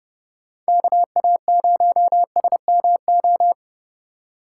Morse code, heard as a keyed tone, about 30 wpm, 700 Hz. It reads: KA0SMO
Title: USA Call Signs - 6 Characters - 3x Character Spacing - 30wpm